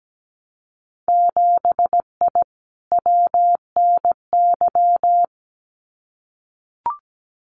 Morse code, heard as a keyed tone, about 17 wpm, 700 Hz. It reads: 7I WNY
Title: Winter Field Day Exchanges Encore 17wpm